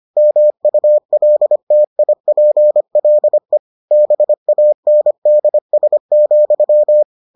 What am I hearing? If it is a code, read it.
Morse code, heard as a keyed tone, about 25 wpm, 600 Hz. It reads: MULTIPLE BANDS,